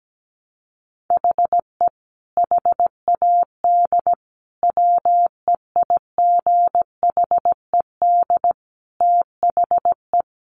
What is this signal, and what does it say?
Morse code, heard as a keyed tone, about 17 wpm, 700 Hz. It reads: HE HAD WEIGHED THE
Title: A Sound Of Thunder 17wpm